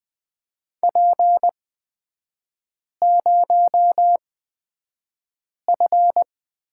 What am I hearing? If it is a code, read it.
Morse code, heard as a keyed tone, about 20 wpm, spaced as for 5 wpm, 700 Hz. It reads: P0F
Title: 255 Lesson 41e - Period - Three Characters 20wpm